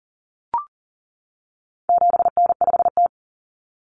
Morse code, heard as a keyed tone, about 40 wpm, 700 Hz. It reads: 7N5T
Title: Common Cut Numbers - 4-Digit 40wpm